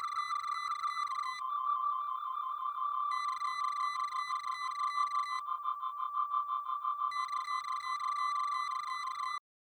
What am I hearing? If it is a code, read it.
There is no code.